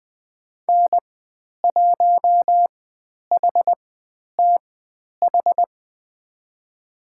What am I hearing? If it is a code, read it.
Morse code, heard as a keyed tone, about 20 wpm, spaced as for 10 wpm, 700 Hz. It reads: N1HTH
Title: 109 Lesson 18g - Letter m - Call Signs 20wpm